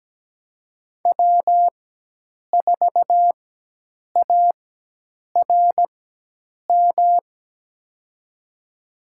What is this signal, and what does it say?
Morse code, heard as a keyed tone, about 17 wpm, spaced as for 8 wpm, 700 Hz. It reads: W4ARM
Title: USA Call Signs - 5 Characters - 4x Character Spacing - 17wpm